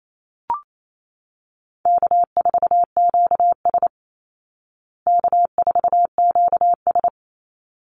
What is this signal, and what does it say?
Morse code, heard as a keyed tone, about 28 wpm, 700 Hz. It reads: K4QS K4QS
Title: Call Signs Repeated 3x - 28wpm